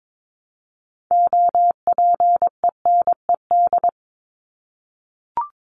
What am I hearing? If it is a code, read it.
Morse code, heard as a keyed tone, about 22 wpm, 700 Hz. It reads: OPENED
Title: ED Suffix - Top 100 Words with Base Words 22wpm